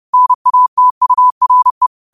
Morse code, heard as a keyed tone, about 30 wpm, 1000 Hz. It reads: NATURE
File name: qotdfiles / 30wpm